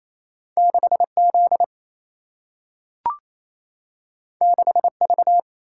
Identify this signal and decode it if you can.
Morse code, heard as a keyed tone, about 28 wpm, 700 Hz. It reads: 6Z 6V
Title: Single-Letter Single-Number 28wpm